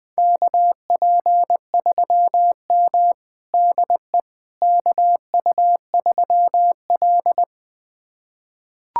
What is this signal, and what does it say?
Morse code, heard as a keyed tone, about 20 wpm, 700 Hz. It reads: KP3M DE KU3L
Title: USA Callsign DE Callsign - 4 Characters - 20wpm